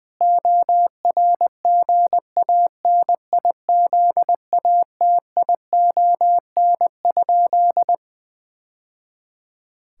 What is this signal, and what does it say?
Morse code, heard as a keyed tone, about 20 wpm, 700 Hz. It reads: ORGANIZATION?
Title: Sentences from Top 1000 Words - Review Part 5 - 20wpm